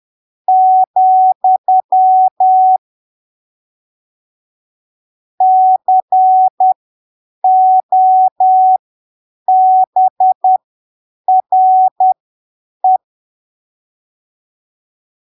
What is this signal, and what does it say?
Morse code, heard as a keyed tone, about 10 wpm, spaced as for 7 wpm, 750 Hz. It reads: , COBRE
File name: arquivos / texto / texto 12 10ppm